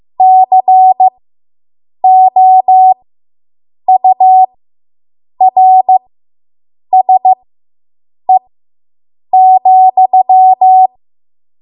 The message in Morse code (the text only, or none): COURSE,